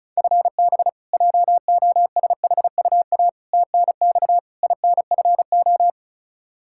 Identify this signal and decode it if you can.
Morse code, heard as a keyed tone, about 34 wpm, 700 Hz. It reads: FB JOSHUA TNX INFO
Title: K1USN SST QSO Practice 35wpm